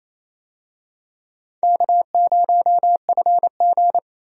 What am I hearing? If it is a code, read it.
Morse code, heard as a keyed tone, about 28 wpm, 700 Hz. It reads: K0FG
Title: USA Call Signs - 4 Characters - 28wpm